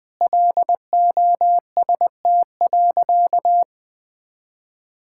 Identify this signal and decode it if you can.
Morse code, heard as a keyed tone, about 20 wpm, 700 Hz. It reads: LOST.
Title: Sentences with 7-Word Phrases - 20wpm